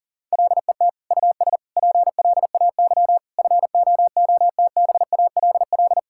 Morse code, heard as a keyed tone, about 37 wpm, 700 Hz. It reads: LET US PLAY FOOTBALL
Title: Sentences from Top 1600 Words 40wpm